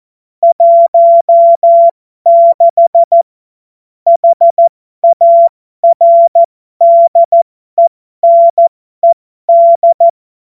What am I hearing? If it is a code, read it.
Morse code, heard as a keyed tone, about 14 wpm, spaced as for 12 wpm, 680 Hz. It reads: 16 HARDENED
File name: Morsekurs / Witze